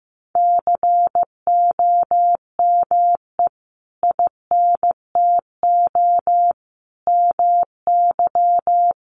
Morse code, heard as a keyed tone, about 15 wpm, 700 Hz. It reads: COME INTO MY
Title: Sentences from Top 600 Words 15wpm